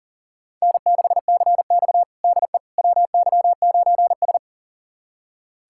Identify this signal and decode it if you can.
Morse code, heard as a keyed tone, about 40 wpm, 700 Hz. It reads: N6CX DE WY9S
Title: USA Callsign DE Callsign - 4 Characters - 40wpm